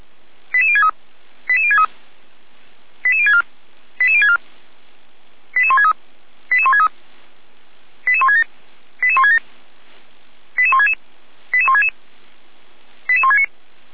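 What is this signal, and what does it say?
Five-tone selective call (ZVEI), heard as quick runs of five tones, tones of about 70 ms. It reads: 88063 88063 88064 88064 88163 88163 88167 88167 88166 88166 88169
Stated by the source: ZVEI five-tone selective call received on an SDR.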